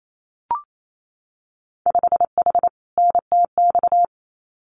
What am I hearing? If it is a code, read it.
Morse code, heard as a keyed tone, about 28 wpm, 700 Hz. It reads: 5H NTX
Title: Winter Field Day Exchanges Encore 28wpm